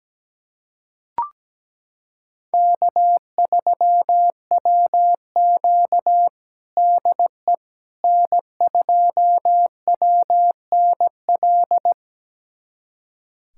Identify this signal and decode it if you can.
Morse code, heard as a keyed tone, about 17 wpm, 700 Hz. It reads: K3WQ DE N2WNL
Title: USA Callsign DE Callsign 17wpm